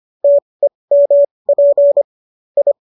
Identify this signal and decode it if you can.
Morse code, heard as a keyed tone, about 25 wpm, spaced as for 20 wpm, 570 Hz. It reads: TEMP I
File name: qso214 20